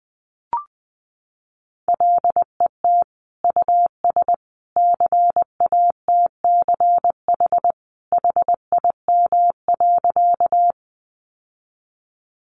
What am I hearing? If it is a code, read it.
Morse code, heard as a keyed tone, about 20 wpm, 700 Hz. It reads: LET US CATCH HIM.
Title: Sentences from Top 1500 Words 20wpm